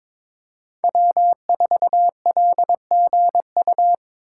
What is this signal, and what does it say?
Morse code, heard as a keyed tone, about 22 wpm, 700 Hz. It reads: W4LGU
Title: Code Groups - 5 Letter:Numbers - 22wpm